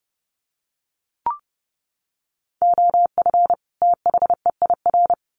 Morse code, heard as a keyed tone, about 30 wpm, 700 Hz. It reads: OF THEIR
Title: Sets of 2 Words 30wpm